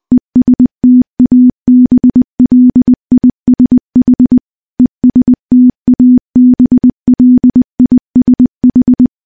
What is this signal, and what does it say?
Morse code, heard as a keyed tone, about 20 wpm, 250 Hz. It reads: ESTABLISH ESTABLISH
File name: morse-training-20WPM